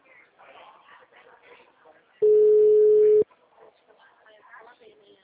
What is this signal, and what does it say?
Ringback tone (Europe, 425 Hz), with speech.
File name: out- -2015- - - .43246